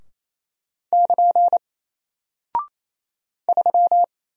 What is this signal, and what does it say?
Morse code, heard as a keyed tone, about 27 wpm, 700 Hz. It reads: <KN> 3
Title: Single Character (Letter:Number:Punctuation:Prosign) Alt 28wpm